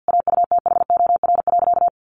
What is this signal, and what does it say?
Morse code, heard as a keyed tone, about 50 wpm, 700 Hz. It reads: AUTHOR.
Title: Sentences from Top 1200 Words 50wpm